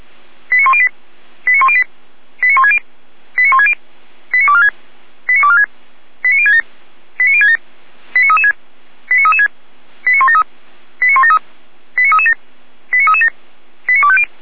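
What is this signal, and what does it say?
Five-tone selective call (ZVEI), heard as quick runs of five tones, tones of about 70 ms. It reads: 89108 89108 89160 89160 89246 89246 89067 89067 89336 89336 89173 89173 89337 89337 89250
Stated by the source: ZVEI five-tone call received on SDR.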